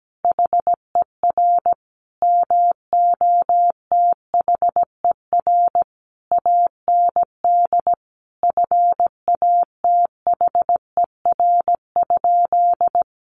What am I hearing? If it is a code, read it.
Morse code, heard as a keyed tone, about 17 wpm, 700 Hz. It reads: HER MOTHER AND FATHER?
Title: Sentences from Top 1200 Words 17wpm